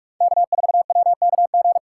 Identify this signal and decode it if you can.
Morse code, heard as a keyed tone, about 45 wpm, 700 Hz. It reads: K4WKG
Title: USA Call Signs - 5 Characters - 2x Character Spacing - 45wpm